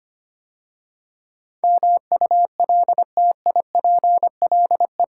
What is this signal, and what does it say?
Morse code, heard as a keyed tone, about 25 wpm, 700 Hz. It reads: MULTIPLE
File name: 8 Letter Words - First 3 Letters Given - 25wpm